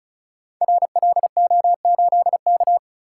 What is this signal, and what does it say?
Morse code, heard as a keyed tone, about 36 wpm, 700 Hz. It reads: RLO8K
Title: Code Groups - 5 Letter:Numbers - 35wpm